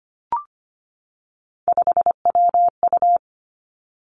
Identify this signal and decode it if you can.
Morse code, heard as a keyed tone, about 25 wpm, 700 Hz. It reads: 5WU
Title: Three Challenging Letter-Number Combinations 25wpm